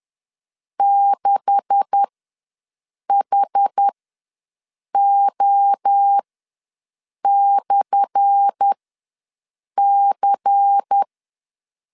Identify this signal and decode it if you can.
Morse code, heard as a keyed tone, about 10 wpm, spaced as for 6 wpm, 800 Hz. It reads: 6HO/C
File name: LEZIONI / 10 SIGNS / 10A SLASH ws 10wpm 4121